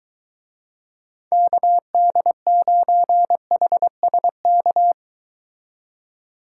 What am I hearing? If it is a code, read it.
Morse code, heard as a keyed tone, about 23 wpm, 700 Hz. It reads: KD9HSK